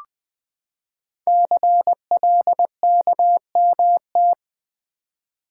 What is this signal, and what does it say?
Morse code, heard as a keyed tone, about 20 wpm, 700 Hz. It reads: CLKMT
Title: Code Groups - 5 Letters - 20wpm